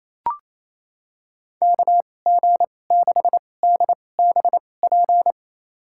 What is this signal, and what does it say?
Morse code, heard as a keyed tone, about 28 wpm, spaced as for 20 wpm, 700 Hz. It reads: KG6DBP
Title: USA Call Signs - 6 Characters - 2x Character Spacing - 28wpm